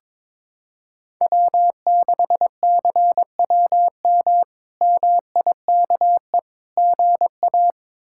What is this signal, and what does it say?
Morse code, heard as a keyed tone, about 22 wpm, 700 Hz. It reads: W6CWM MIKE GA
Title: K1USN SST QSO Practice - 22wpm